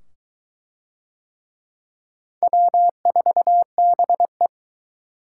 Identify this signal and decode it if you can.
Morse code, heard as a keyed tone, about 22 wpm, 700 Hz. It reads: W4BE